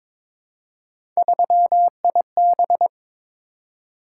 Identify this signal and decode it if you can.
Morse code, heard as a keyed tone, about 22 wpm, 700 Hz. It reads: 3IB
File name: practice / 22wpm / Three Letter-Number Combinations - 22wpm